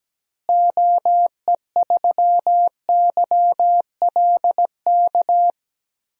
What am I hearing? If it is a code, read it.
Morse code, heard as a keyed tone, about 17 wpm, 700 Hz. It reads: OE3YLK
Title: DXCC-100 Call Signs - 6 Characters - Repeated 3x - 17wpm